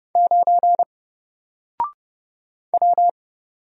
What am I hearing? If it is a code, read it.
Morse code, heard as a keyed tone, about 30 wpm, 700 Hz. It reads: 9 W
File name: Single-Letter-Number-30wpm